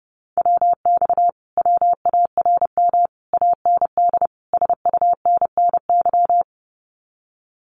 Morse code, heard as a keyed tone, about 30 wpm, 700 Hz. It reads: WX WARM AND SUNNY
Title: Common QSO Phrases Encore 30wpm